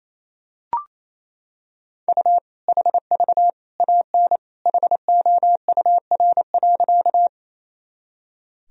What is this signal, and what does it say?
Morse code, heard as a keyed tone, about 28 wpm, 700 Hz. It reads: U HV AN HOUR.
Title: Sentences from Top 400 Words and Abbreviations 28wpm